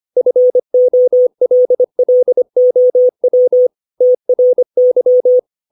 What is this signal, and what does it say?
Morse code, heard as a keyed tone, about 25 wpm, 500 Hz. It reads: FOLLOW TRY
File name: 632 words / 632 Words 25WPM